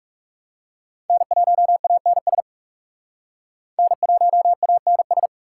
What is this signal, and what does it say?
Morse code, heard as a keyed tone, about 35 wpm, 700 Hz. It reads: N1ANS N1ANS
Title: Speed Racing - USA 1x3 Call Signs 30wpm